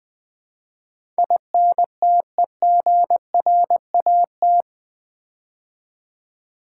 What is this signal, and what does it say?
Morse code, heard as a keyed tone, about 20 wpm, 700 Hz. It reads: INTEGRAT
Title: Build A Word - 4 Letters At A Time - 10-Letter Words - 20wpm